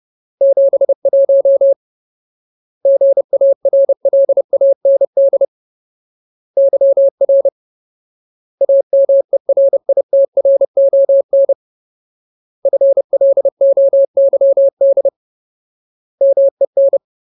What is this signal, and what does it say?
Morse code, heard as a keyed tone, about 30 wpm, 550 Hz. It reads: 71 GARLAND YR AMERITRON FLOYD MEN
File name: Ultimate Word List / 30WPM Ultimate